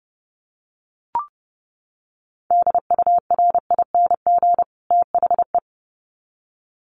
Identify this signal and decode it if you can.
Morse code, heard as a keyed tone, about 30 wpm, 700 Hz. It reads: DURING THE
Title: Sets of 2 Words 30wpm